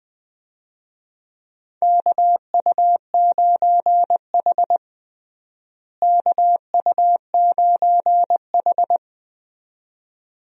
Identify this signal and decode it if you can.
Morse code, heard as a keyed tone, about 20 wpm, 700 Hz. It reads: KU9H KU9H
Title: USA Call Signs - 4 Characters - Repeated 2x - 20wpm